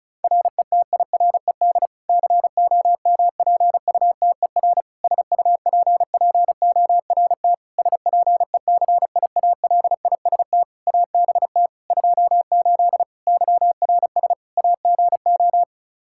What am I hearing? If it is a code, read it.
Morse code, heard as a keyed tone, about 34 wpm, 700 Hz. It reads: RETIRED COMPUTER SUPPORT SPECIALIST ABT 28 YRS AGO
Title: Practice QSO - Exchange 3 - More Info - 35wpm